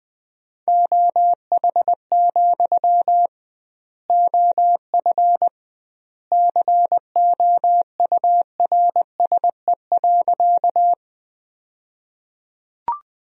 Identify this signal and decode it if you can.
Morse code, heard as a keyed tone, about 20 wpm, 700 Hz. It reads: OH, OF COURSE.
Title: 2x Word-Spacing - Sentences from Top 500 Words - 20wpm